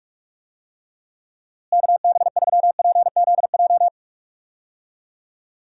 Morse code, heard as a keyed tone, about 41 wpm, 700 Hz. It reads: KB3PZJ
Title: USA Call Signs - 6 Characters - Repeated 2x - 45wpm